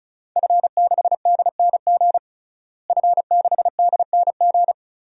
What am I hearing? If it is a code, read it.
Morse code, heard as a keyed tone, about 36 wpm, 700 Hz. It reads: F6DNG F6DNG
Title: DXCC-100 Call Signs - 5 Characters - Repeated 3x - 35wpm